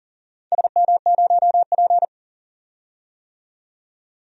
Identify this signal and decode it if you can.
Morse code, heard as a keyed tone, about 40 wpm, 700 Hz. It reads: SM0P
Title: DXCC-100 Call Signs - 4 Characters - 40wpm